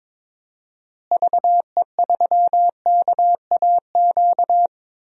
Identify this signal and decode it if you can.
Morse code, heard as a keyed tone, about 21 wpm, 700 Hz. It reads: VE3KAQ
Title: DXCC-100 Call Signs - 6 Characters - Repeated 2x - 22wpm